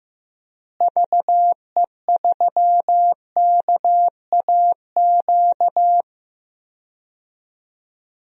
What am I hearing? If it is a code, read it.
Morse code, heard as a keyed tone, about 15 wpm, 700 Hz. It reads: VE3KAQ
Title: DXCC-100 Call Signs - 6 Characters - Repeated 2x - 15wpm